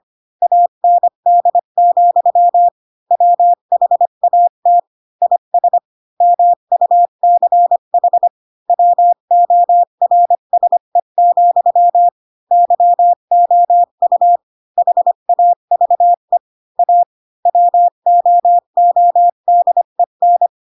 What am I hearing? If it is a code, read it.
Morse code, heard as a keyed tone, about 25 wpm, 700 Hz. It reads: AND, WHAT IS MUCH WORSE, YOU HAVE A WOODEN
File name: Pinocchio / Pinocchio ch4 20 25 ik1hge